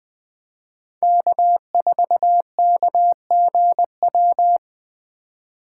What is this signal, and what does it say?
Morse code, heard as a keyed tone, about 20 wpm, 700 Hz. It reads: K4KGW